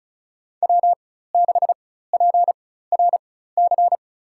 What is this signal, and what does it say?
Morse code, heard as a keyed tone, about 36 wpm, spaced as for 16 wpm, 700 Hz. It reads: W6PRC
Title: USA Call Signs - 4x Character Spacing - 35wpm